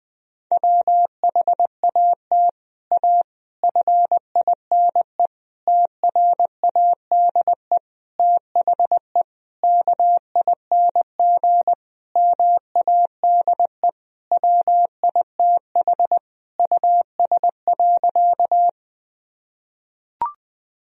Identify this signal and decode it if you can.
Morse code, heard as a keyed tone, about 20 wpm, 700 Hz. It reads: WHAT A FINE TRADE THE KING MADE WITH US.
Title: Sentences from Top 1000 Words - Review Part 4 - 20wpm